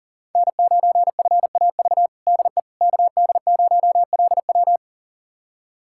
Morse code, heard as a keyed tone, about 40 wpm, 700 Hz. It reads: N9FAV DE KD0LW
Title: USA Callsign DE Callsign - 5 Characters - 40wpm